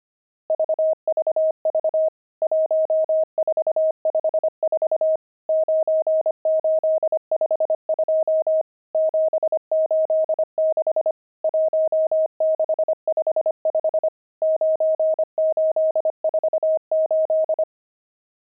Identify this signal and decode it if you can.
Morse code, heard as a keyed tone, about 25 wpm, 630 Hz. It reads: VVV 1454 9852 786 1655 9848
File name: pangram-fs-25-cs-25-nums17